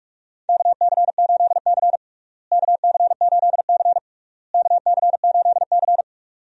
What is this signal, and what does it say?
Morse code, heard as a keyed tone, about 46 wpm, 700 Hz. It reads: KC8C KC8C KC8C
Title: USA Call Signs - 4 Characters - Repeated 3x - 45wpm